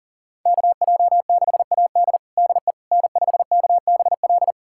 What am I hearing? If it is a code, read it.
Morse code, heard as a keyed tone, about 40 wpm, 700 Hz. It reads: KJ6AD DE N5KBL
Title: USA Callsign DE Callsign - 5 Characters - 40wpm